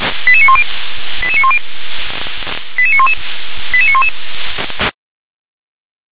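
Five-tone selective call (ZVEI), heard as quick runs of five tones, tones of about 70 ms. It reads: 88910 88910 88911 88911
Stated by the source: SDR reception of ZVEI five-tone call 88742.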